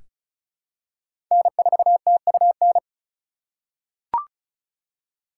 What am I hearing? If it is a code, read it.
Morse code, heard as a keyed tone, about 36 wpm, 700 Hz. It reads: N4TUN
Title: USA Call Signs - 5 Characters - 2x Character Spacing - 35wpm